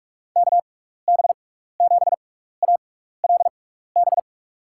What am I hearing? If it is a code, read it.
Morse code, heard as a keyed tone, about 42 wpm, spaced as for 15 wpm, 700 Hz. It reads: KB7ALB
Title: USA Call Signs - 6 Characters - 6x Character Spacing - 45wpm